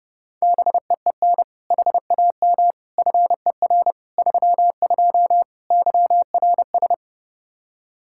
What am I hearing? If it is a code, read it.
Morse code, heard as a keyed tone, about 32 wpm, 700 Hz. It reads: BEEN HAM FER 32 YRS
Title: K1USN SST QSO Practice 30wpm